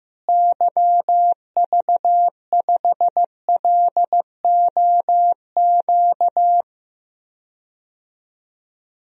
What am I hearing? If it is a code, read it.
Morse code, heard as a keyed tone, about 15 wpm, 700 Hz. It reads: YV5LOQ
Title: DXCC-100 Call Signs - 6 Characters - Repeated 2x - 15wpm